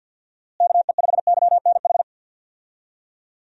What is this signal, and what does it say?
Morse code, heard as a keyed tone, about 52 wpm, 700 Hz. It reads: KE5YNH
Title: USA Call Signs - 6 Characters - Repeated 3x - 50wpm